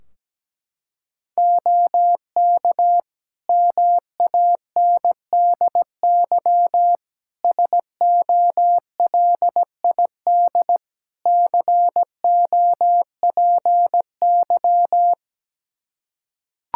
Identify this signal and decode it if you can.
Morse code, heard as a keyed tone, about 17 wpm, 700 Hz. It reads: OK MANDY SOLID COPY
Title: Practice QSO - Exchange 3 - More Info - 17wpm